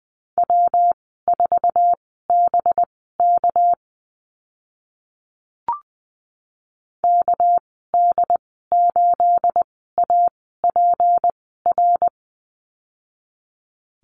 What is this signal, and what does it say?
Morse code, heard as a keyed tone, about 20 wpm, spaced as for 14 wpm, 700 Hz. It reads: W4BK KD8APR
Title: 219 Lesson 35g - Number 0 - Call Signs 20wpm